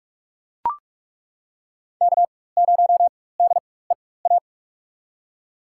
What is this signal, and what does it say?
Morse code, heard as a keyed tone, about 41 wpm, spaced as for 20 wpm, 700 Hz. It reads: K0DEA
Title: USA Call Signs - 5 Characters - 4x Character Spacing - 45wpm